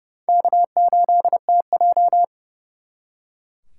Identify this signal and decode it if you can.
Morse code, heard as a keyed tone, about 37 wpm, 700 Hz. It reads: K8TJ
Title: Speed Racing - USA 1x2 Call Signs 30wpm